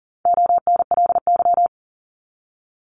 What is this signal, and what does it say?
Morse code, heard as a keyed tone, about 40 wpm, 700 Hz. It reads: ONLY
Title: Top 200 Words 40wpm